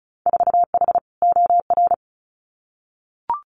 Morse code, heard as a keyed tone, about 36 wpm, 700 Hz. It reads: 4H OR
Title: Winter Field Day Exchanges Encore 35wpm